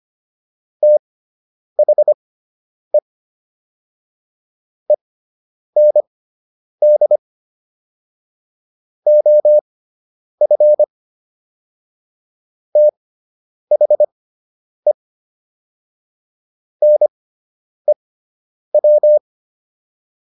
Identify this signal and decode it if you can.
Morse code, heard as a keyed tone, about 25 wpm, spaced as for 9 wpm, 600 Hz. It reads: THE END OF THE NEW